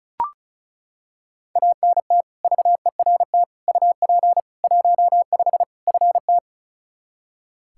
Morse code, heard as a keyed tone, about 34 wpm, 700 Hz. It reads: ANT VERT UP 15 FT
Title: K1USN SST QSO Practice 35wpm